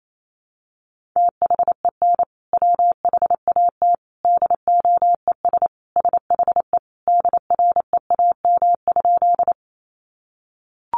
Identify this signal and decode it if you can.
Morse code, heard as a keyed tone, about 28 wpm, 700 Hz. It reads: THEN WHAT DOES SHE DREAM?
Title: Sentences from Top 1500 Words 28wpm